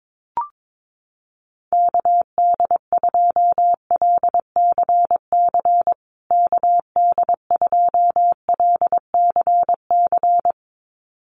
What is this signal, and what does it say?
Morse code, heard as a keyed tone, about 22 wpm, 700 Hz. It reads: KD2LCC KD2LCC
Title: USA Call Signs Repeated 2x 22wpm